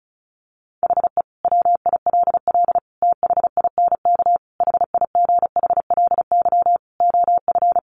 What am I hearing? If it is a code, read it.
Morse code, heard as a keyed tone, about 34 wpm, 700 Hz. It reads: HE WILL THINK HIGHLY OF
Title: Sentences from Top 1100 Words 35wpm